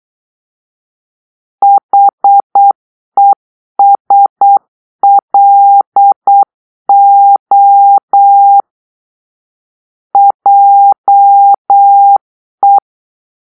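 Morse code, heard as a keyed tone, about 8 wpm, 800 Hz. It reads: HESLO JE
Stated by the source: Morse-code cipher made by a player team for the Kachna puzzle game.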